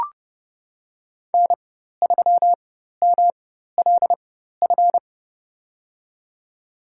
Morse code, heard as a keyed tone, about 30 wpm, spaced as for 14 wpm, 700 Hz. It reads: N3MLF
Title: USA Call Signs - 5 Characters - 4x Character Spacing - 30wpm